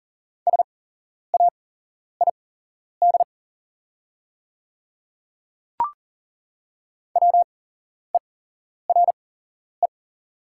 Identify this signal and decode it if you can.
Morse code, heard as a keyed tone, about 40 wpm, spaced as for 11 wpm, 690 Hz. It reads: SAID WERE
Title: 8x Character Spacing - Top 100 Words - 40wpm